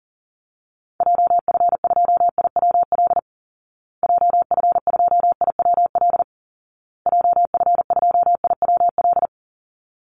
Morse code, heard as a keyed tone, about 40 wpm, 700 Hz. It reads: JF2IWL JF2IWL JF2IWL
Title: Call Signs Repeated 2x - 40wpm